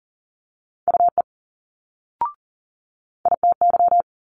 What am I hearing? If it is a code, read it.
Morse code, heard as a keyed tone, about 40 wpm, 700 Hz. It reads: UE ITY
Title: Multi-letter Phonemes Quick Fire 40wpm